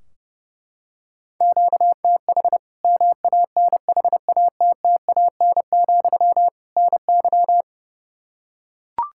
Morse code, heard as a keyed tone, about 32 wpm, 700 Hz. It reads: QTH MANHATTAN, NY
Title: Practice QSO - Exchange 1 - Beginning - 30wpm